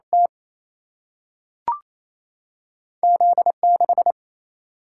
Morse code, heard as a keyed tone, about 28 wpm, 700 Hz. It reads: T Z6
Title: Single-Letter Single-Number 28wpm